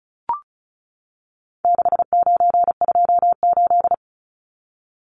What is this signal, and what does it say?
Morse code, heard as a keyed tone, about 34 wpm, 700 Hz. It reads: 6928
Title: Numbers - 4-Digit 35wpm